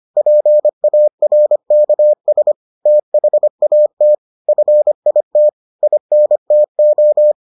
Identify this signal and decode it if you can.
Morse code, heard as a keyed tone, about 25 wpm, 600 Hz. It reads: PARKS THAT FIT INTO